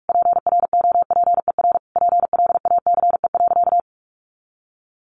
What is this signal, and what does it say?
Morse code, heard as a keyed tone, about 44 wpm, 700 Hz. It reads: PROPER PLACE.
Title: Sentences from Top 800 Words 45wpm